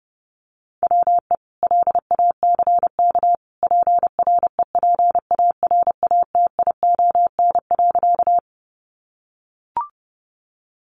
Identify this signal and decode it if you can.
Morse code, heard as a keyed tone, about 28 wpm, 700 Hz. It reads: WE LACK PREPARATION.
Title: Sentences from Top 1400 Words 30wpm